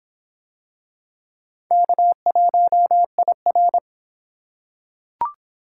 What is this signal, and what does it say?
Morse code, heard as a keyed tone, about 25 wpm, 700 Hz. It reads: K1IR